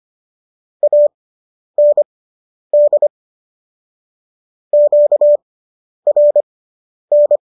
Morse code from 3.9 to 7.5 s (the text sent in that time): QRN